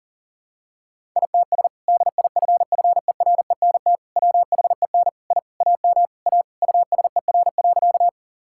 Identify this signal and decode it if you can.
Morse code, heard as a keyed tone, about 40 wpm, 700 Hz. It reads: ITS DIFFERENT WHEN I AM A USER.
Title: Sentences from Top 1000 Words - Review Part 5 - 40wpm